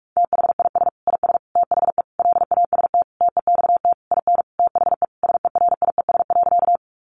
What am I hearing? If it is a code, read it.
Morse code, heard as a keyed tone, about 44 wpm, 700 Hz. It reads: THIS IS THE LAST TEXT IN THE SERIES.
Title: Sentences from Top 1200 Words 45wpm